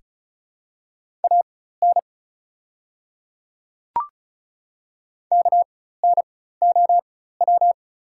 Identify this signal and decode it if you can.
Morse code, heard as a keyed tone, about 36 wpm, spaced as for 16 wpm, 700 Hz. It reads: AN KNOW
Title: 4x Character Spacing - Top 100 Words - 35wpm